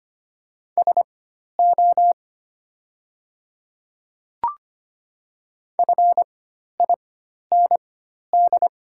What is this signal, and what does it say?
Morse code, heard as a keyed tone, about 25 wpm, spaced as for 12 wpm, 700 Hz. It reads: SO FIND
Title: 4x Character Spacing - Top 100 Words - 25wpm